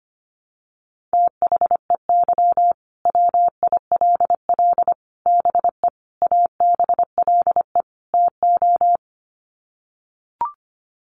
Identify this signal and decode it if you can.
Morse code, heard as a keyed tone, about 25 wpm, 690 Hz. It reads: THEY WILL BE ABLE TO
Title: Sets of 5 Words Encore 25wpm